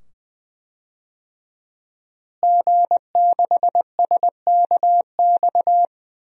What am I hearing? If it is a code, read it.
Morse code, heard as a keyed tone, about 21 wpm, 700 Hz. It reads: G6SKX
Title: DXCC-100 Call Signs - 5 Characters - Repeated 2x - 20wpm